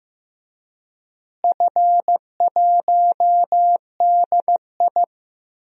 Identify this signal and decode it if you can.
Morse code, heard as a keyed tone, about 15 wpm, 700 Hz. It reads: F1DI
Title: DXCC-100 Call Signs - 4 Characters - Repeated 3x - 15wpm